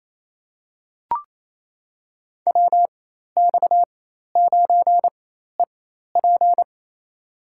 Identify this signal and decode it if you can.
Morse code, heard as a keyed tone, about 28 wpm, spaced as for 13 wpm, 700 Hz. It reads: WX9EP
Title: USA Call Signs - 5 Characters - 4x Character Spacing - 28wpm